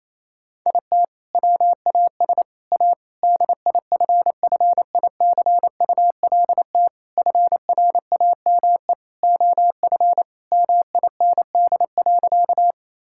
Morse code, heard as a keyed tone, about 28 wpm, 700 Hz. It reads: IT WAS A DIFFICULT FRAME OF MIND.
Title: Sentences from Top 1000 Words - Review Part 6 - 28wpm